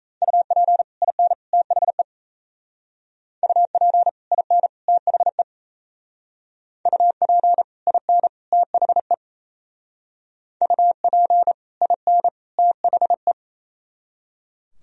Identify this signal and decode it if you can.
Morse code, heard as a keyed tone, about 33 wpm, 700 Hz. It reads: UP IN THE UP IN THE UP IN THE UP IN THE
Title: Speed-Racing - Sets of 3 Words 28wpm